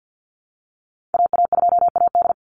Morse code, heard as a keyed tone, about 50 wpm, 700 Hz. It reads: AA2AD
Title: Call Signs Repeated 2x - 50wpm